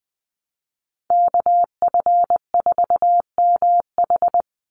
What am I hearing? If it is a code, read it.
Morse code, heard as a keyed tone, about 20 wpm, 700 Hz. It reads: KF4MH
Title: NA SOTA Chaser Callsigns 20wpm